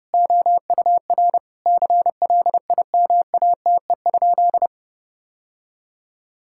Morse code, heard as a keyed tone, about 32 wpm, 700 Hz. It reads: OUR CLIMATE?
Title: Sentences from Top 1600 Words 30wpm